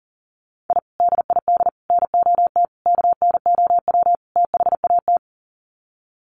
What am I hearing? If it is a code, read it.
Morse code, heard as a keyed tone, about 40 wpm, 700 Hz. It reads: I DID NOT KNOW THAT
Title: Sets of 5 Words 40wpm